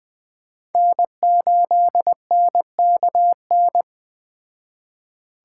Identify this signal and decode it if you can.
Morse code, heard as a keyed tone, about 21 wpm, 700 Hz. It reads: N8NKN
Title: Speed Racing - USA 1x3 Call Signs 17wpm